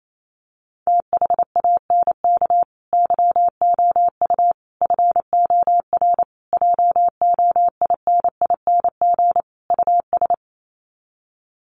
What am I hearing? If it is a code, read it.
Morse code, heard as a keyed tone, about 28 wpm, 700 Hz. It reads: THANK YOU FOR JOINING US
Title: Sets of 5 Words 28wpm